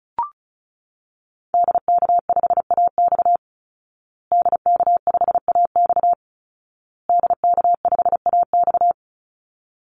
Morse code, heard as a keyed tone, about 34 wpm, 700 Hz. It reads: DK5AX DK5AX DK5AX
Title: Call Signs Repeated 2x - 35wpm